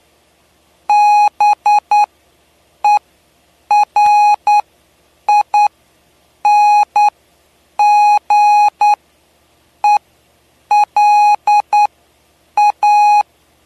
Morse code, heard as a keyed tone, about 9 wpm, spaced as for 7 wpm, 830 Hz. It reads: BERINGELA
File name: cw / py2lcd / 07-PY2LCD-Li��o 7